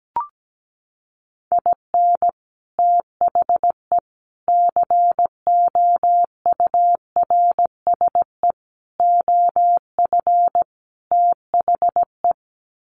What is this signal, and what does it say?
Morse code, heard as a keyed tone, about 17 wpm, 700 Hz. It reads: IN THE COURSE OF THE
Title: Sets of 5 Words 17wpm